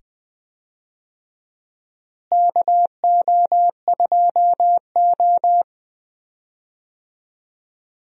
Morse code, heard as a keyed tone, about 20 wpm, 700 Hz. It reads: KO2O
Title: USA Call Signs - 4 Characters - Repeated 2x - 20wpm